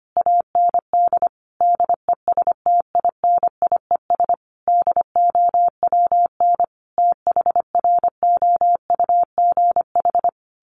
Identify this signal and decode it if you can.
Morse code, heard as a keyed tone, about 25 wpm, 700 Hz. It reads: AND DESTINIES DOWN THROUGH
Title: A Sound Of Thunder 25wpm